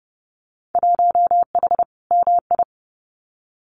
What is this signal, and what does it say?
Morse code, heard as a keyed tone, about 30 wpm, 700 Hz. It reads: 1H MI
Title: Winter Field Day Exchanges 30wpm